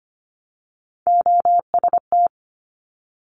Morse code, heard as a keyed tone, about 25 wpm, 700 Hz. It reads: OST
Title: Top Three Letter Combinations in Words 25wpm